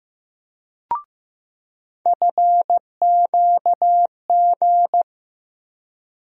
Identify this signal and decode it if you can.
Morse code, heard as a keyed tone, about 15 wpm, 700 Hz. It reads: FQG
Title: USA Call Sign Postfixes - 3 Letters 15wpm